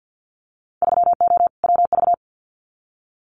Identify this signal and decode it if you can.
Morse code, heard as a keyed tone, about 50 wpm, 700 Hz. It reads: 3O WV
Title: Winter Field Day Exchanges 50wpm